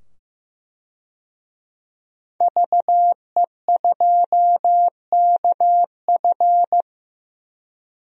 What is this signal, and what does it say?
Morse code, heard as a keyed tone, about 15 wpm, 700 Hz. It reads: VE2KF
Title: DXCC-100 Call Signs - 5 Characters - 15wpm